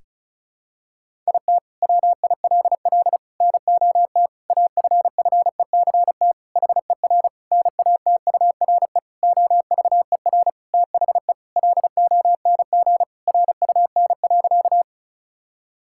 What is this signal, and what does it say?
Morse code, heard as a keyed tone, about 34 wpm, 700 Hz. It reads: IT WILL NOT AFFECT HER NATURE OVER THE LONG RUN.
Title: Sentences from Top 1000 Words - Review Part 6 - 35wpm